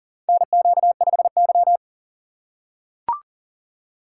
Morse code, heard as a keyed tone, about 40 wpm, 700 Hz. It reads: NQ5Y
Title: USA Call Signs - 6x Character Spacing - 40wpm